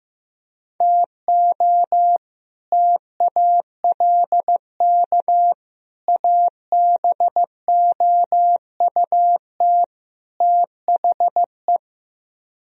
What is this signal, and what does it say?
Morse code, heard as a keyed tone, about 15 wpm, 700 Hz. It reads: TO TALK ABOUT THE
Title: Speed-Racing - Sets of 4 Words 15wpm